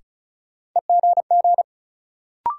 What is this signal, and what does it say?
Morse code, heard as a keyed tone, about 34 wpm, 700 Hz. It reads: EGG
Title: Sentences from Top 1000 Words - Review Part 7 - New Word Drill - 35wpm